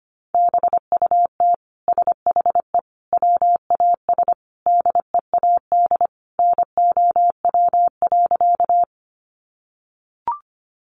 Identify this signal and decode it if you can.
Morse code, heard as a keyed tone, about 25 wpm, 700 Hz. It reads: BUT SHE WAS DEAD NOW.
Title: Sentences from Top 1500 Words 25wpm